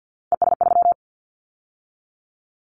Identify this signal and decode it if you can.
Morse code, heard as a keyed tone, about 50 wpm, 700 Hz. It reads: ES3
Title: Callsign Prefixes for Top 50 DXCC - 50wpm